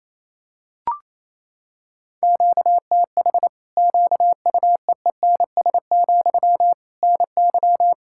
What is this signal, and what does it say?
Morse code, heard as a keyed tone, about 28 wpm, 700 Hz. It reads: QTH QUEENS, NY
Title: Practice QSO - Exchange 1 - Beginning - 28wpm